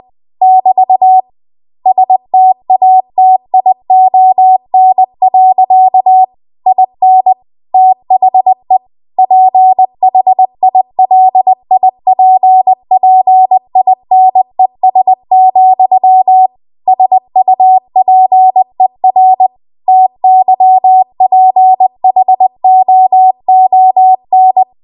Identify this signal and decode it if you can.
Morse code, heard as a keyed tone, about 20 wpm, 750 Hz. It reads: <BT> STATION. IN THE PHILIPPINES, SUPER TYPHOON
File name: Archive / 20 WPM / 20WPM